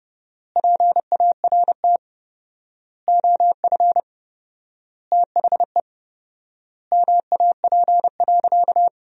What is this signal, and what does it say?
Morse code, heard as a keyed tone, about 32 wpm, 700 Hz. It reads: PART OF THE MAP.
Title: 4x Word-Spacing - Sentences from Top 400 Words - 30wpm